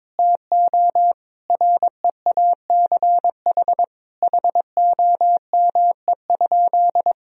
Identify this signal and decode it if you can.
Morse code, heard as a keyed tone, about 22 wpm, 700 Hz. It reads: TO REACH HOME?
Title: Sentences from Top 1000 Words - Review Part 2 - 22wpm